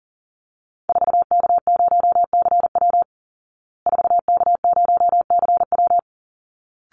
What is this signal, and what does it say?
Morse code, heard as a keyed tone, about 37 wpm, 700 Hz. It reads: 4K0CW 4K0CW
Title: Call Signs Repeated 3x - 40wpm